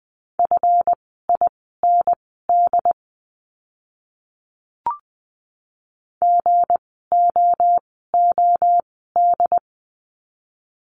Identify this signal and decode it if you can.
Morse code, heard as a keyed tone, about 20 wpm, spaced as for 15 wpm, 700 Hz. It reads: FIND GOOD
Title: 160 Lesson 26f - Letter g - Words 20wpm